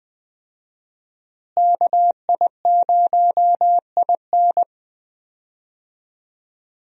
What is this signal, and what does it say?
Morse code, heard as a keyed tone, about 20 wpm, 700 Hz. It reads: KI0IN
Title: USA Call Signs - 5 Characters - Repeated 2x - 20wpm